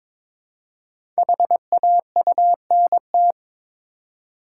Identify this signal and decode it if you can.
Morse code, heard as a keyed tone, about 22 wpm, 700 Hz. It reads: HAUNT
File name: 5 Letter Words - First 3 Letters Given - 22wpm